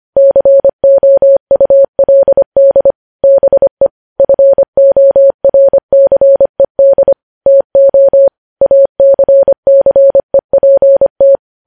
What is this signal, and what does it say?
Morse code, heard as a keyed tone, about 25 wpm, 550 Hz. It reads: COULD BE FORCED TO ACCEPT
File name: podcast / News headlines-1x-25x25